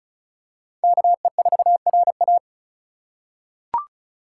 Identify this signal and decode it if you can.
Morse code, heard as a keyed tone, about 36 wpm, 700 Hz. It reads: KE4RA
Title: USA Call Signs - 2x Character Spacing - 35wpm